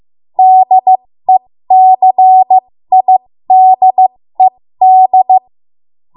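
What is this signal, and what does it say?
Morse code, heard as a keyed tone, about 15 wpm, spaced as for 13 wpm, 750 Hz. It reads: DECIDED
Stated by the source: 13 wpm Morse code practice archive transmission.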